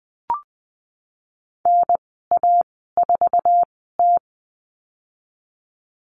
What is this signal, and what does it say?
Morse code, heard as a keyed tone, about 20 wpm, spaced as for 15 wpm, 700 Hz. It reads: NA4T
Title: 048 Lesson 9g - Number 4 - Call Signs 20wpm